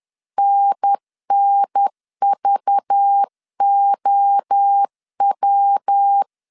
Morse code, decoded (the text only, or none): NNVOW